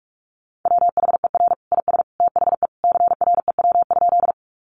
Morse code, heard as a keyed tone, about 44 wpm, 700 Hz. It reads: WHER IS THE CREW?
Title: Sentences from Top 1500 Words 45wpm